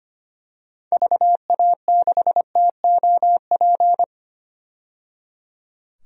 Morse code, heard as a keyed tone, about 25 wpm, 700 Hz. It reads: VA6TOP
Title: DXCC-100 Call Signs - 6 Characters - 25wpm